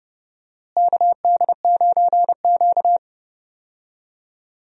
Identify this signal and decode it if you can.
Morse code, heard as a keyed tone, about 30 wpm, 700 Hz. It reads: KD9Q
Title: USA Call Signs - 4 Characters - Repeated 3x - 30wpm